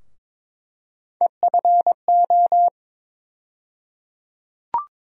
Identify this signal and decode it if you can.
Morse code, heard as a keyed tone, about 22 wpm, 700 Hz. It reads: EFO
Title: USA Call Sign Postfixes - 3 Letters 22wpm